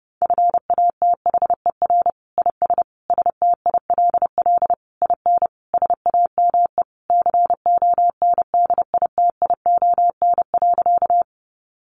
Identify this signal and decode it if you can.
Morse code, heard as a keyed tone, about 30 wpm, 700 Hz. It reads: FATHER IS STILL IN SAME CONDITION.
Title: Sentences from Top 1000 Words 30wpm